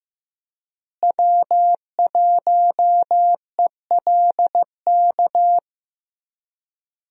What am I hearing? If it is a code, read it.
Morse code, heard as a keyed tone, about 15 wpm, 700 Hz. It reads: W1ELK
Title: USA Call Signs - 5 Characters - Repeated 3x - 15wpm